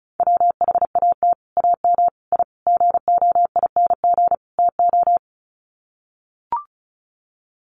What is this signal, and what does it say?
Morse code, heard as a keyed tone, about 34 wpm, 700 Hz. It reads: WHAT AM I GOING TO
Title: Sets of 5 Words Encore 35wpm